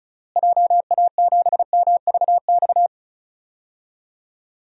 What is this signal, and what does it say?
Morse code, heard as a keyed tone, about 34 wpm, 700 Hz. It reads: JA7MVX
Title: DXCC-100 Call Signs - 6 Characters - Repeated 3x - 35wpm